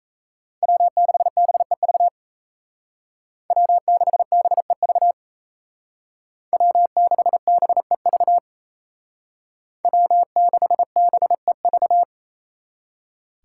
Morse code, heard as a keyed tone, about 35 wpm, 700 Hz. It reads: W6BEV W6BEV W6BEV W6BEV
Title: Speed Racing - USA 1x3 Call Signs 28wpm